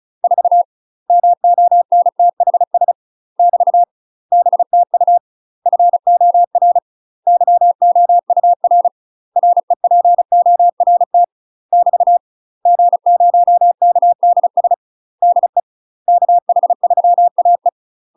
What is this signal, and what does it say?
Morse code, heard as a keyed tone, about 34 wpm, 700 Hz. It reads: 4 MONTHS <BT> BTU FOR YOUR REPORT <BT> G0KDS DE KH3AE
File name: UK MASTER QSOs TRAINING/35WPM